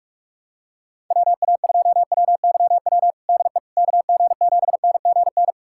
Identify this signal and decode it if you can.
Morse code, heard as a keyed tone, about 46 wpm, 700 Hz. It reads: WA2WYW DE KG7NGN
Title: USA Callsign DE Callsign - 6 Characters - 45wpm